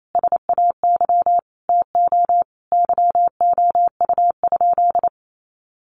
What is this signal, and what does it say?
Morse code, heard as a keyed tone, about 28 wpm, 700 Hz. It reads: SAY TO YOU?
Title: Sentences from Top 800 Words 28wpm